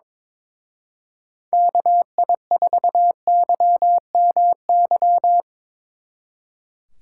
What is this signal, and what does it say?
Morse code, heard as a keyed tone, about 22 wpm, 700 Hz. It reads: KI4YMY
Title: Speed Racing - USA 2x3 Call Signs 22wpm